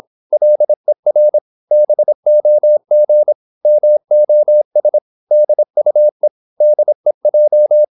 Morse code, heard as a keyed tone, about 26 wpm, 600 Hz. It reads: LER BOG MOS DUE DEJ
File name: Korte ord DK 26wpm